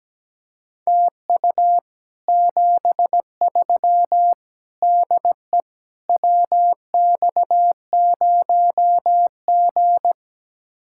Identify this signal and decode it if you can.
Morse code, heard as a keyed tone, about 17 wpm, 700 Hz. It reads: TU 73 DE WX0G
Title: Speed-Racing Common QSO Phrases - 17wpm